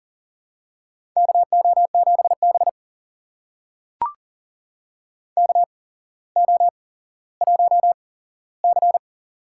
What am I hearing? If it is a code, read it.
Morse code, heard as a keyed tone, about 40 wpm, spaced as for 17 wpm, 700 Hz. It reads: KO7B KO1C
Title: USA Call Signs - 4 Characters - 8x Character Spacing - 40wpm